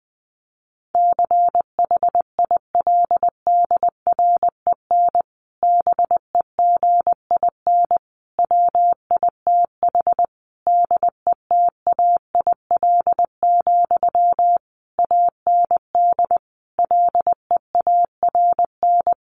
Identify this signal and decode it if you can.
Morse code, heard as a keyed tone, about 20 wpm, 700 Hz. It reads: CHILDREN BEGIN WITH DETAIL, AND LEARN
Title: Sentences from Top 1500 Words 20wpm